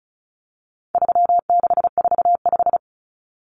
Morse code, heard as a keyed tone, about 34 wpm, 700 Hz. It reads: 3645
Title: Common Cut Numbers - 4-Digit 35wpm